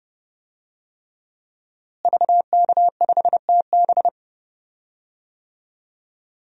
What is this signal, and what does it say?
Morse code, heard as a keyed tone, about 31 wpm, 700 Hz. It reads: VK5TB
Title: DXCC-100 Call Signs - 5 Characters - Repeated 3x - 30wpm